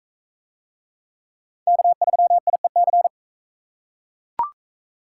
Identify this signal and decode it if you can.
Morse code, heard as a keyed tone, about 42 wpm, 700 Hz. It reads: K3IEC